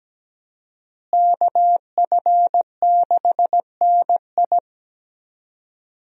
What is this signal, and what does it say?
Morse code, heard as a keyed tone, about 17 wpm, 700 Hz. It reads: KF6NI
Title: USA Call Signs - 5 Characters - Repeated 3x - 17wpm